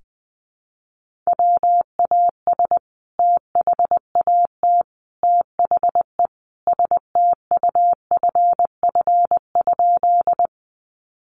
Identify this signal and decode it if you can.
Morse code, heard as a keyed tone, about 19 wpm, 700 Hz. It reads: WAS THAT THE STUFF?
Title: Sentences from Top 1400 Words 20wpm